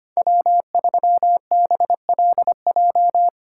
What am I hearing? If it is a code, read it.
Morse code, heard as a keyed tone, about 25 wpm, 700 Hz. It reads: W3BLJ
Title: USA Call Signs - 5 Characters - 2x Character Spacing - 25wpm